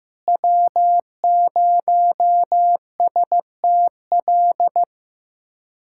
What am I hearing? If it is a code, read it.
Morse code, heard as a keyed tone, about 15 wpm, 700 Hz. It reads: W0STL
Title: USA Callsign DE Callsign - 5 Characters - 15wpm